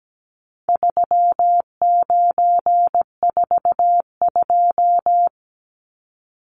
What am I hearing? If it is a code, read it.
Morse code, heard as a keyed tone, about 17 wpm, 700 Hz. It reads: 3942
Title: Numbers - 4-Digit 17wpm